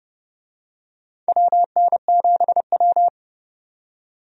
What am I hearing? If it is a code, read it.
Morse code, heard as a keyed tone, about 30 wpm, 700 Hz. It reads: WN7W
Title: USA Call Signs - 4 Characters - 30wpm